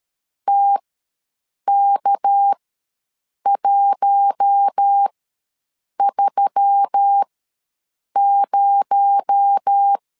Morse code, heard as a keyed tone, about 13 wpm, spaced as for 7 wpm, 800 Hz. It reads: TK130
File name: LEZIONI / 6 PKX0 / 6 PKX0 ws 12wpm 3448